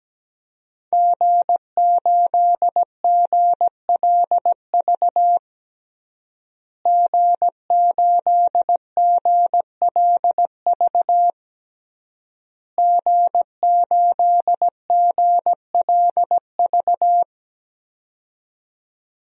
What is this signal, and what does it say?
Morse code, heard as a keyed tone, about 17 wpm, 700 Hz. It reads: G8GLV G8GLV G8GLV
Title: DXCC-100 Call Signs - 5 Characters - Repeated 3x - 17wpm